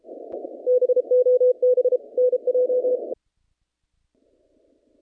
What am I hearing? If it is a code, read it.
Morse code, heard as a keyed tone, about 30 wpm, 500 Hz. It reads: BOB NJ
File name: contests / naqp-cw-k5tr-op-k5pi- -0000